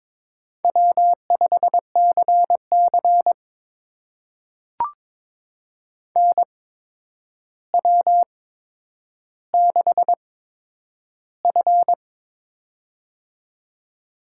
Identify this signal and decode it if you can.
Morse code, heard as a keyed tone, about 23 wpm, spaced as for 10 wpm, 700 Hz. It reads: W5CC NW6F
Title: USA Call Signs - 4 Characters - 8x Character Spacing - 22wpm